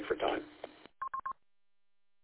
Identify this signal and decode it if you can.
Speech. There is no code.